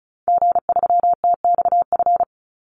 Morse code, heard as a keyed tone, about 34 wpm, 700 Hz. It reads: G3TXF
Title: Call Signs Repeated 3x - 35wpm